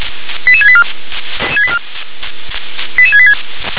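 Five-tone selective call (ZVEI), heard as quick runs of five tones, tones of about 70 ms. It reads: 88684 88686
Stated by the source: ZVEI five-tone selective call received on SDR.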